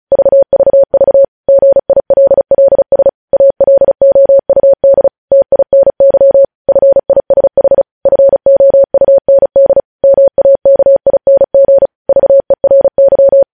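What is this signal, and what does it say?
Morse code, heard as a keyed tone, about 36 wpm, 550 Hz. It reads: VVV GILLS ALOUD TINY FISH FOUND MAKING VERY
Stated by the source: Morse code practice reading news headlines at 35 words per minute.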